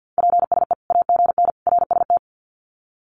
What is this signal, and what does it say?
Morse code, heard as a keyed tone, about 50 wpm, 700 Hz. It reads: PSE AGN RST
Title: Common QSO Phrases 50wpm